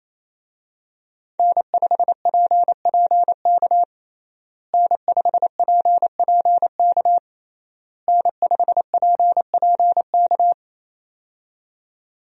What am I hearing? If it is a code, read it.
Morse code, heard as a keyed tone, about 27 wpm, 700 Hz. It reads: N5PPK N5PPK N5PPK
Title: USA Call Signs - 5 Characters - Repeated 3x - 28wpm